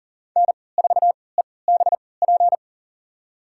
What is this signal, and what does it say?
Morse code, heard as a keyed tone, about 40 wpm, spaced as for 23 wpm, 700 Hz. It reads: N4EBP
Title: USA Call Signs - 3x Character Spacing - 40wpm